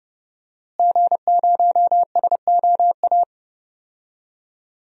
Morse code, heard as a keyed tone, about 30 wpm, 700 Hz. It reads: G0SOA
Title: DXCC-100 Call Signs - 5 Characters - Repeated 3x - 30wpm